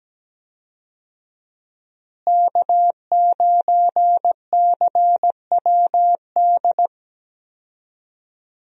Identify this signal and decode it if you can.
Morse code, heard as a keyed tone, about 17 wpm, 700 Hz. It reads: K9CWD
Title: USA Call Signs - 5 Characters - Repeated 2x - 17wpm